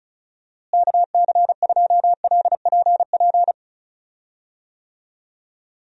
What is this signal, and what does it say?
Morse code, heard as a keyed tone, about 36 wpm, 700 Hz. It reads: KC2LPP
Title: USA Call Signs - 6 Characters - Repeated 3x - 35wpm